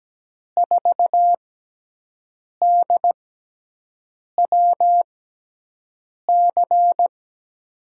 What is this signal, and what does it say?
Morse code, heard as a keyed tone, about 17 wpm, spaced as for 6 wpm, 700 Hz. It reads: 4DWC
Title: USA Call Signs - 5 Characters - 6x Character Spacing - 17wpm